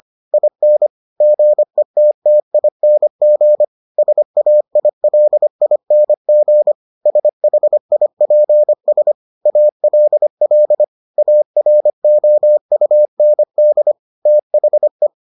Morse code, heard as a keyed tone, about 25 wpm, 600 Hz. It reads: IN GETTING SAILING SHIPS ALL AROUND THE